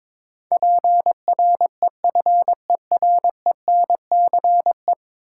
Morse code, heard as a keyed tone, about 22 wpm, 700 Hz. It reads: PREFERENCE
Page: morse-code-ninja-practice-22wpm